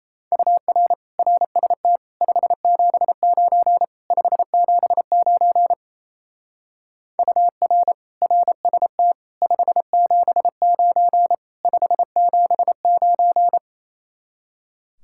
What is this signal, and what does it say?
Morse code, heard as a keyed tone, about 35 wpm, 700 Hz. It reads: UR RST 579 579 UR RST 579 579